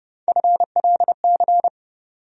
Morse code, heard as a keyed tone, about 30 wpm, 700 Hz. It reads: FLC
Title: USA Call Sign Postfixes - 3 Letters 30wpm